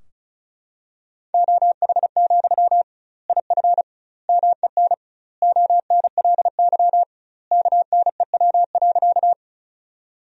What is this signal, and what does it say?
Morse code, heard as a keyed tone, about 34 wpm, 700 Hz. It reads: OH, IF MEN ONLY KNEW.
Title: 2x Word-Spacing - Sentences from Top 500 Words - 35wpm